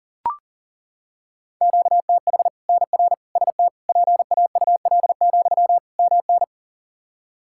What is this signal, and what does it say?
Morse code, heard as a keyed tone, about 39 wpm, 700 Hz. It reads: QTH NR ST PAUL, MN
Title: K1USN SST QSO Practice 40wpm